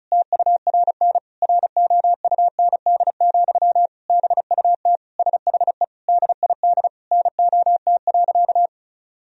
Morse code, heard as a keyed tone, about 36 wpm, 700 Hz. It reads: TURN ROUND, BUT SHE DID NOT.
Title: Sentences from Top 1000 Words - Review Part 1 - 35wpm